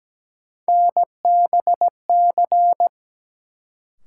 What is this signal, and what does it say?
Morse code, heard as a keyed tone, about 17 wpm, 700 Hz. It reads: NBC
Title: Sets of 5 Words Encore - New Word Drill - 17wpm